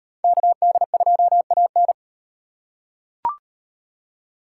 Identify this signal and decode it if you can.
Morse code, heard as a keyed tone, about 38 wpm, 700 Hz. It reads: KD2AN